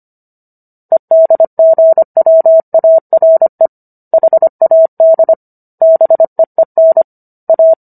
Morse code, heard as a keyed tone, about 25 wpm, 650 Hz. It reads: EDGWARE HAD BEEN A